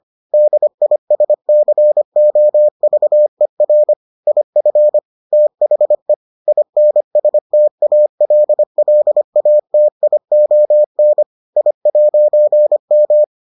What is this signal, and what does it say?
Morse code, heard as a keyed tone, about 25 wpm, 600 Hz. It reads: DISCOVER IF THE INSTALLATION I'M